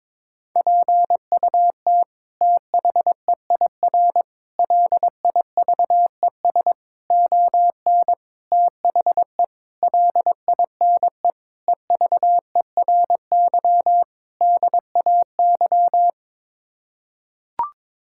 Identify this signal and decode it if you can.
Morse code, heard as a keyed tone, about 22 wpm, 700 Hz. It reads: PUT THEIR LIVES ON THE LINE EVERY DAY
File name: practice / 22wpm / Sets of 8 Words - 22wpm